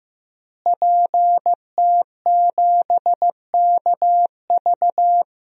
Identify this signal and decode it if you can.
Morse code, heard as a keyed tone, about 15 wpm, 700 Hz. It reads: PT7KV
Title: DXCC-100 Call Signs - 5 Characters - Repeated 3x - 15wpm